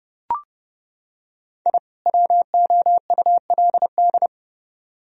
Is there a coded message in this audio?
Yes: Morse code from 1.7 to 4.3 s, heard as a keyed tone, about 30 wpm, 700 Hz. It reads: I WOULD